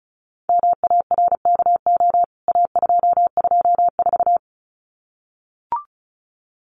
Morse code, heard as a keyed tone, about 34 wpm, 700 Hz. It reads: MARKO A224
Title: CWT Exchange 35wpm